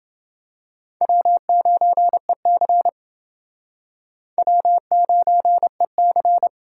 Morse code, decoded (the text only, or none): W9EC W9EC